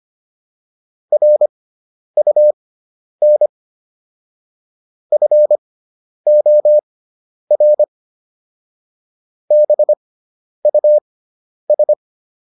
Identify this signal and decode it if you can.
Morse code, heard as a keyed tone, about 25 wpm, spaced as for 10 wpm, 600 Hz. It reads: RUN FOR BUS